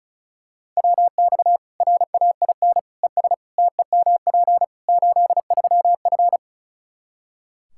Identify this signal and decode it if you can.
Morse code, heard as a keyed tone, about 34 wpm, 700 Hz. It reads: WX RAIN ES TEMP 83F
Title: Practice QSO - Exchange 2 - Info - 35wpm